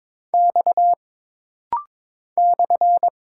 Morse code, heard as a keyed tone, about 22 wpm, 700 Hz. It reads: X /
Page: This rapid-fire timing is geared to push your ICR (Instant Character Recognition) skills.